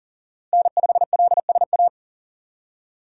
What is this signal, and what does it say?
Morse code, heard as a keyed tone, about 40 wpm, 700 Hz. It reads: N5LSA
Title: USA Call Signs - 5 Characters - 3x Character Spacing - 40wpm